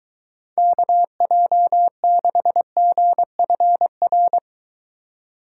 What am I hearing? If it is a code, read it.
Morse code, heard as a keyed tone, about 23 wpm, 700 Hz. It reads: KJ6GFR